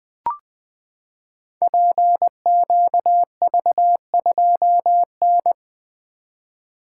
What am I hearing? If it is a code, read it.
Morse code, heard as a keyed tone, about 20 wpm, 700 Hz. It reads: PQV2N
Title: Code Groups - 5 Letter:Numbers - 20wpm